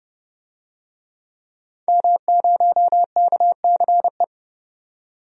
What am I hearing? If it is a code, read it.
Morse code, heard as a keyed tone, about 30 wpm, 700 Hz. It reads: M0KCE
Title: DXCC-100 Call Signs - 5 Characters - 30wpm